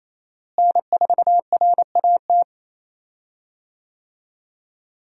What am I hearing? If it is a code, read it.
Morse code, heard as a keyed tone, about 27 wpm, 700 Hz. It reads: N4RAT
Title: USA Call Signs - 5 Characters - Repeated 2x - 28wpm